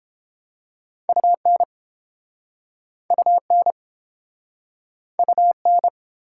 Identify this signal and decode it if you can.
Morse code, heard as a keyed tone, about 32 wpm, 700 Hz. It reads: UN UN UN